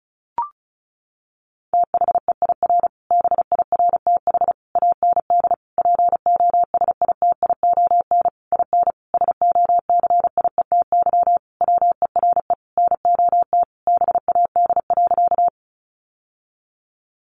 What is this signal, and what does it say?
Morse code, heard as a keyed tone, about 34 wpm, 700 Hz. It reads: THEIR BIRTH AND POSITION IN SOCIETY WERE NOT BAD.
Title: Sentences from Top 900 Words 35wpm